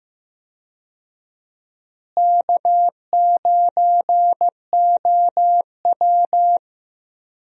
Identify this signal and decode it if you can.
Morse code, heard as a keyed tone, about 15 wpm, 700 Hz. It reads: K9OW
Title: USA Call Signs - 4 Characters - Repeated 3x - 15wpm